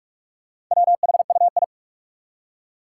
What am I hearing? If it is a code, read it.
Morse code, heard as a keyed tone, about 46 wpm, 700 Hz. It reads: WHUI
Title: Four Challenging Letter-Number Combinations 45wpm